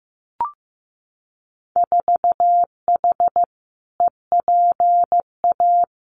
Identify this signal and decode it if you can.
Morse code, heard as a keyed tone, about 15 wpm, 700 Hz. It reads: 4H EPA
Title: Winter Field Day Exchanges 15wpm